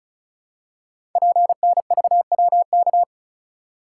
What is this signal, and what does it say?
Morse code, heard as a keyed tone, about 36 wpm, 700 Hz. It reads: PNVWK
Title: Code Groups - 5 Letters - 35wpm